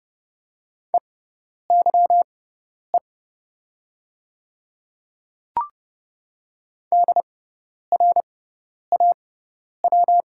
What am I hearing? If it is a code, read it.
Morse code, heard as a keyed tone, about 30 wpm, spaced as for 10 wpm, 690 Hz. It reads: EYE DRAW
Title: 6x Character Spacing - Top 300 Words - 30wpm